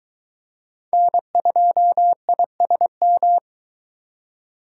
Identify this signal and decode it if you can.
Morse code, heard as a keyed tone, about 22 wpm, 700 Hz. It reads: N2ISM